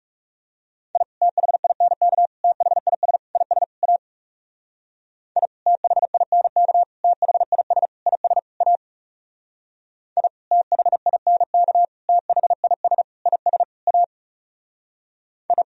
Morse code, heard as a keyed tone, about 38 wpm, 700 Hz. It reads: I THINK THIS IS A I THINK THIS IS A I THINK THIS IS A I